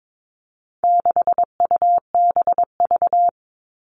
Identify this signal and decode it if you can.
Morse code, heard as a keyed tone, about 22 wpm, 700 Hz. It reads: 6UBV
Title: All Cut Numbers - 4-Digit 22wpm